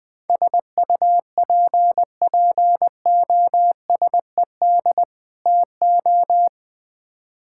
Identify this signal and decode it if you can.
Morse code, heard as a keyed tone, about 21 wpm, 700 Hz. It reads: SUPPOSED TO
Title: Build A Phrase - 3-Word Phrases - 20wpm